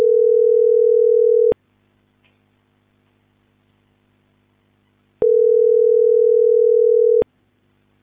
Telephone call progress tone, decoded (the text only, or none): ringback tone